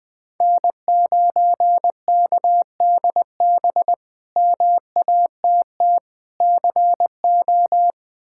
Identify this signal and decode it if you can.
Morse code, heard as a keyed tone, about 20 wpm, 700 Hz. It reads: N9KDB MATT CO
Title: K1USN SST QSO Practice - 20wpm